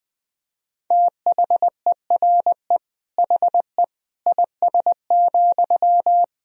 Morse code, heard as a keyed tone, about 20 wpm, 700 Hz. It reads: THERE HE IS,
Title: Sentences from Top 100 Words and Abbreviations 20wpm